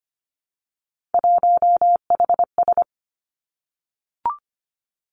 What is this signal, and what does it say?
Morse code, heard as a keyed tone, about 25 wpm, 700 Hz. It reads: 1HS
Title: Three Challenging Letter-Number Combinations 25wpm